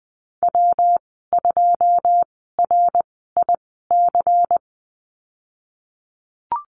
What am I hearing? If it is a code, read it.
Morse code, heard as a keyed tone, about 20 wpm, spaced as for 15 wpm, 700 Hz. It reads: W2RIC
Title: 179 Lesson 29g - Slash - Call Signs 20wpm